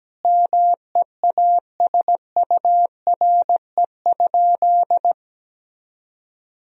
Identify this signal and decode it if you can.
Morse code, heard as a keyed tone, about 17 wpm, 700 Hz. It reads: MEASURE?
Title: Sentences from Top 1000 Words - Review Part 3 - 17wpm